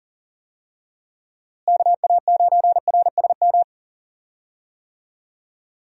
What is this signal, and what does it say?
Morse code, heard as a keyed tone, about 40 wpm, 700 Hz. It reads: KA9RSM
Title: USA Call Signs - 6 Characters - Repeated 3x - 40wpm